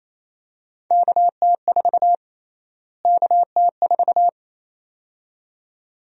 Morse code, heard as a keyed tone, about 28 wpm, 700 Hz. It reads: KT4 KT4
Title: USA Call Sign Prefixes - Repeated 2x - 28wpm